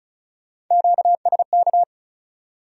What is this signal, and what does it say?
Morse code, heard as a keyed tone, about 47 wpm, 700 Hz. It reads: QSK